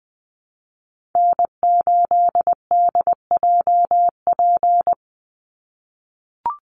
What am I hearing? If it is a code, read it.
Morse code, heard as a keyed tone, about 20 wpm, 700 Hz. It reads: N8DJP
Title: USA Call Signs 20wpm